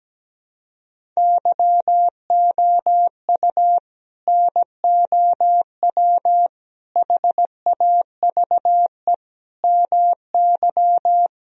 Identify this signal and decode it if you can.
Morse code, heard as a keyed tone, about 17 wpm, 700 Hz. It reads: YOU NOW HAVE MY
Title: Sentences from Top 1000 Words - Review Part 1 - 17wpm